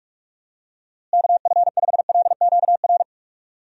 Morse code, heard as a keyed tone, about 45 wpm, 700 Hz. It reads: KF5LQR
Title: USA Call Signs - 6 Characters - Repeated 3x - 45wpm